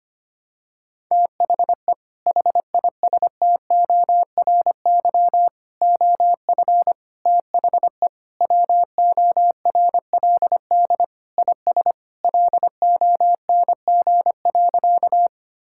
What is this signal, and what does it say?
Morse code, heard as a keyed tone, about 25 wpm, 700 Hz. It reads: THE HISTORY OF THE WORLD IS LONG.
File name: practice / 25wpm / Sentences with 3-Word Phrases - 25wpm